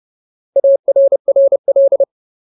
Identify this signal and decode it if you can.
Morse code, heard as a keyed tone, about 30 wpm, 550 Hz. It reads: ARRL
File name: Ultimate Word List / 30WPM Ultimate